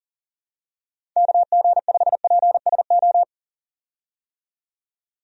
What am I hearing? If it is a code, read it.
Morse code, heard as a keyed tone, about 38 wpm, 700 Hz. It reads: KG5PSO
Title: USA Call Signs - 6 Characters - 40wpm